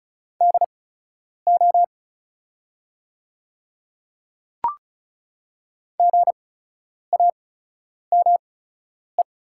Morse code, heard as a keyed tone, about 36 wpm, spaced as for 10 wpm, 690 Hz. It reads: DO GAME
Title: 8x Character Spacing - Top 500 Words - 35wpm